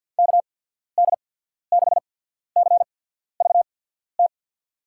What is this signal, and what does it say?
Morse code, heard as a keyed tone, about 50 wpm, spaced as for 14 wpm, 700 Hz. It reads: KD6CVT
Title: USA Call Signs - 6 Characters - 8x Character Spacing - 50wpm